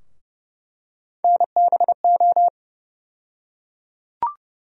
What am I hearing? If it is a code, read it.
Morse code, heard as a keyed tone, about 30 wpm, 700 Hz. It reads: NBO
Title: USA Call Sign Postfixes - 3 Letters 30wpm